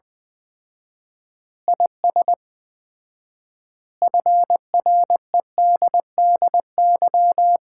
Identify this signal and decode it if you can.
Morse code, heard as a keyed tone, about 21 wpm, 700 Hz. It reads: IS FREDDY
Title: Common QSO Phrases - 4x Word-Spacing - 20wpm